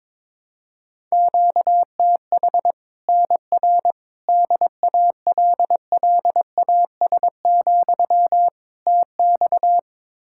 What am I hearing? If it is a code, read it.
Morse code, heard as a keyed tone, about 22 wpm, 700 Hz. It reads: QTH NR DALLAS, TX
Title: Practice QSO - Exchange 1 - Beginning - 22wpm